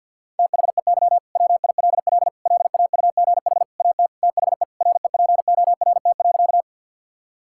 Morse code, heard as a keyed tone, about 50 wpm, 700 Hz. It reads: THEY WILL LAUGH AT THE REPORT.
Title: Sentences from Top 1000 Words - Review Part 6 - 50wpm